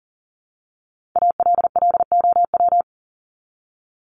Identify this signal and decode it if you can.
Morse code, heard as a keyed tone, about 40 wpm, 700 Hz. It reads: ALLOW
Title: Top 501-750 Words 40wpm